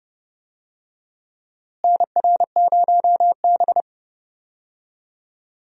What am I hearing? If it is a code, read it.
Morse code, heard as a keyed tone, about 30 wpm, 700 Hz. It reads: NR0B
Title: USA Call Signs - 4 Characters - Repeated 3x - 30wpm